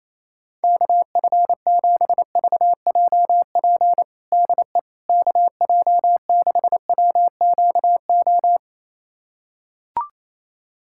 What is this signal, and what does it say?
Morse code, heard as a keyed tone, about 28 wpm, 700 Hz. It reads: KF7VJP DE KJ6WQO
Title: USA Callsign DE Callsign 28wpm